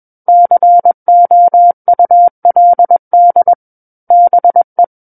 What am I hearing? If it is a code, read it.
Morse code, heard as a keyed tone, about 21 wpm, 700 Hz. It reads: COULD BE
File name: Book One / 12-Chapter 12 - 21 WPM